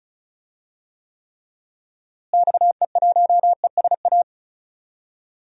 Morse code, heard as a keyed tone, about 36 wpm, 700 Hz. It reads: XE1ESA
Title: DXCC-100 Call Signs - 6 Characters - Repeated 2x - 35wpm